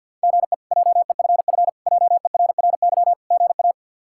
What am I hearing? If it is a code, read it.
Morse code, heard as a keyed tone, about 47 wpm, 700 Hz. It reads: GE JEFF JERRY GA
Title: K1USN SST QSO Practice - 50wpm